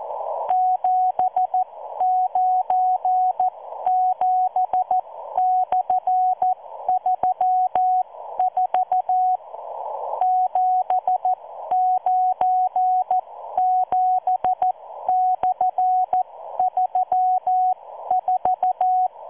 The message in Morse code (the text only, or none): 797/34 797/34